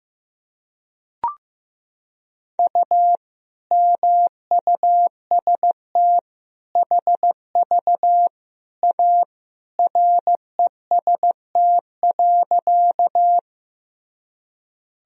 Morse code, heard as a keyed tone, about 15 wpm, 700 Hz. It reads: U MUST HV A REST.
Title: Sentences from Top 500 Words and Abbreviations 15wpm